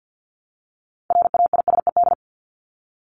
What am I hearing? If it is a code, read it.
Morse code, heard as a keyed tone, about 50 wpm, 700 Hz. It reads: RAISED
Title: ED Suffix - Top 100 Words with Base Words 50wpm